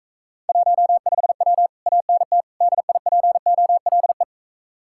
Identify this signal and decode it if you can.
Morse code, heard as a keyed tone, about 43 wpm, 700 Hz. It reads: 15W ANT DIPOLE